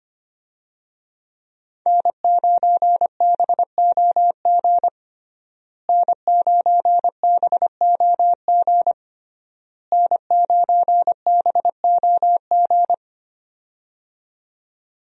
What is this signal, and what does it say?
Morse code, heard as a keyed tone, about 25 wpm, 700 Hz. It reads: N9BOG N9BOG N9BOG
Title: USA Call Signs - 5 Characters - Repeated 3x - 25wpm